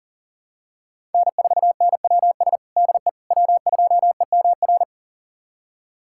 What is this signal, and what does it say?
Morse code, heard as a keyed tone, about 40 wpm, 700 Hz. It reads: N4NWS DE W2EMR
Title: USA Callsign DE Callsign - 5 Characters - 40wpm